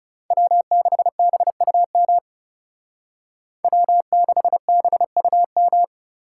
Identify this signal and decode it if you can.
Morse code, heard as a keyed tone, about 38 wpm, 700 Hz. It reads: W6BUM W6BUM